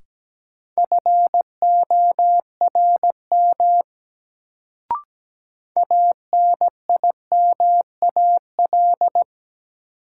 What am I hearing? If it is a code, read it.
Morse code, heard as a keyed tone, about 17 wpm, 700 Hz. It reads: FORM ANIMAL
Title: Sentences from Top 1000 Words - Review Part 1 - New Word Drill - 17wpm